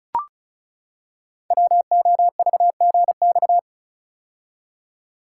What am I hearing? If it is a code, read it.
Morse code, heard as a keyed tone, about 36 wpm, 700 Hz. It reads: WOVGX
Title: Code Groups - 5 Letters - 35wpm